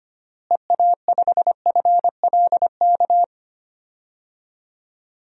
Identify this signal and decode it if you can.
Morse code, heard as a keyed tone, about 25 wpm, 700 Hz. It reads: EA5FLK
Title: DXCC-100 Call Signs - 6 Characters - Repeated 3x - 25wpm